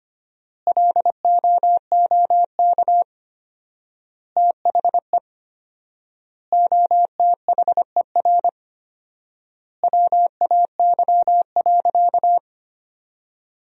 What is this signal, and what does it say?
Morse code, heard as a keyed tone, about 25 wpm, 700 Hz. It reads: LOOK THE OTHER WAY.
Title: 4x Word-Spacing - Sentences from Top 100 Words - 25wpm